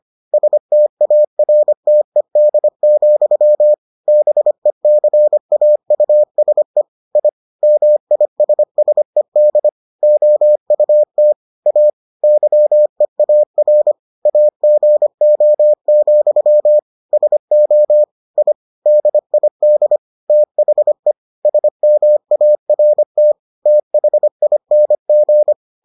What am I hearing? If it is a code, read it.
Morse code, heard as a keyed tone, about 25 wpm, 600 Hz. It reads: STARTED, BECAUSE I MISSED OUT A YEAR AGO, SO I DID THE SMART THING